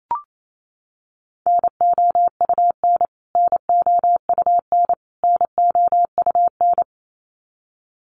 Morse code, heard as a keyed tone, about 28 wpm, 700 Hz. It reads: NOUN NOUN NOUN
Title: Top 500 Words Repeated 28wpm